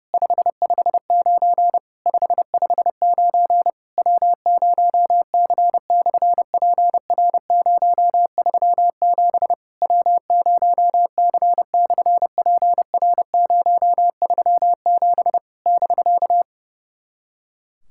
Morse code, heard as a keyed tone, about 30 wpm, 700 Hz. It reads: 559 559 W0C/PR037 W0C/PR037 <BK>
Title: SOTA QSO Practice 30wpm